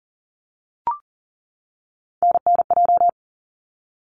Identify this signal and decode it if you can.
Morse code, heard as a keyed tone, about 40 wpm, 700 Hz. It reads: NNJ
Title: ARRL Sections 40wpm